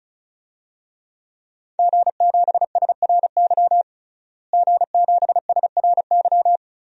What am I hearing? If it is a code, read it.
Morse code, heard as a keyed tone, about 34 wpm, 700 Hz. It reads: G7SRY G7SRY
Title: DXCC-100 Call Signs - 5 Characters - Repeated 2x - 35wpm